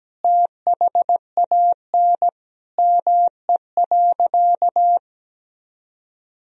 Morse code, heard as a keyed tone, about 17 wpm, 700 Hz. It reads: THAN ME.
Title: Sentences from Top 400 Words and Abbreviations 17wpm